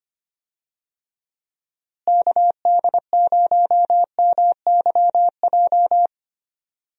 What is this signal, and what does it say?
Morse code, heard as a keyed tone, about 25 wpm, 700 Hz. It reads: KD0MYJ
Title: USA Call Signs - 6 Characters - Repeated 2x - 25wpm